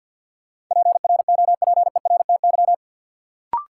Morse code, heard as a keyed tone, about 50 wpm, 700 Hz. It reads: PROPERTY
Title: Sentences from Top 1000 Words - New Word Drill - 50wpm